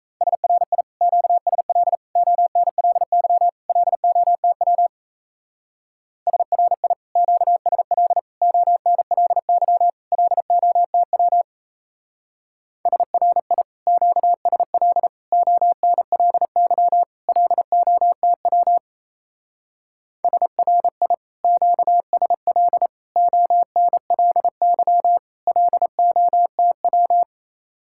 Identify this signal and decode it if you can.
Morse code, heard as a keyed tone, about 33 wpm, 700 Hz. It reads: SRI QSL ONLY LOTW SRI QSL ONLY LOTW SRI QSL ONLY LOTW SRI QSL ONLY LOTW
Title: Speed-Racing Common QSO Phrases Encore - 28wpm